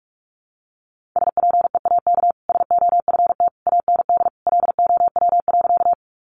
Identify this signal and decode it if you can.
Morse code, heard as a keyed tone, about 44 wpm, 700 Hz. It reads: SPEAK SOFT AND LOW.
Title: Sentences from Top 1500 Words 45wpm